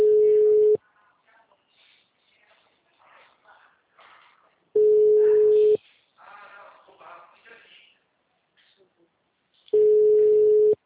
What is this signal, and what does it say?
Ringback tone (Europe, 425 Hz), with speech.